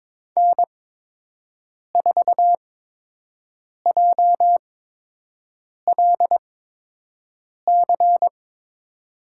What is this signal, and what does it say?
Morse code, heard as a keyed tone, about 22 wpm, spaced as for 6 wpm, 700 Hz. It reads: N4JLC
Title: USA Call Signs - 5 Characters - 8x Character Spacing - 22wpm